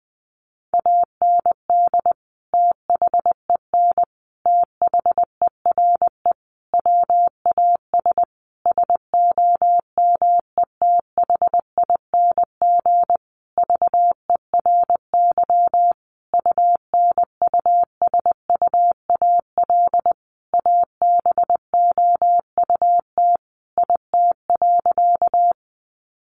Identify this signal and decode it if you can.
Morse code, heard as a keyed tone, about 20 wpm, 700 Hz. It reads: AND THEN THERE WAS SOMETHING VERY UNUSUAL ABOUT IT.
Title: Sentences from Top 1400 Words 20wpm